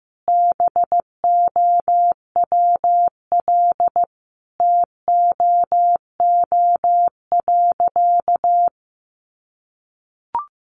Morse code, heard as a keyed tone, about 15 wpm, 700 Hz. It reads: BOWL TOO.
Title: Sentences from Top 1200 Words 15wpm